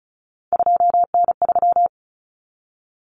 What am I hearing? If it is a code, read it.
Morse code, heard as a keyed tone, about 34 wpm, 700 Hz. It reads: 2N3
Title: Common Cut Numbers - 3-Digit 35wpm